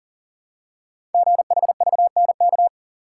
Morse code, heard as a keyed tone, about 42 wpm, 700 Hz. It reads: GHVNK
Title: Code Groups - 5 Letters - 40wpm